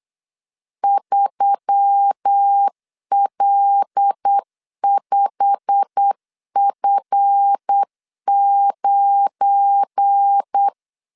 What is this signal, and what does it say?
Morse code, heard as a keyed tone, about 8 wpm, 800 Hz. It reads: 3L5F9
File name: LEZIONI / 8 YJ45 / 8 YJ45 8wpm 3214